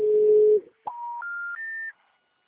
Special information tone.